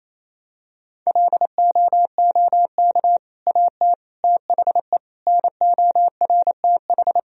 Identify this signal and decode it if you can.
Morse code, heard as a keyed tone, about 28 wpm, 700 Hz. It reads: LOOK AT THE NORTH
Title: Sentences from Top 1000 Words - Review Part 2 - 28wpm